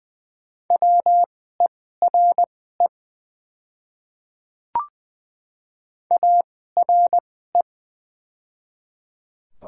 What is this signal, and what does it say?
Morse code, heard as a keyed tone, about 21 wpm, spaced as for 15 wpm, 700 Hz. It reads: WERE ARE
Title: 2x Character Spacing - Top 100 Words - 20wpm